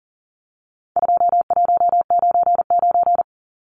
Morse code, heard as a keyed tone, about 40 wpm, 700 Hz. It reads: 2199
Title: Numbers - 4-Digit 40wpm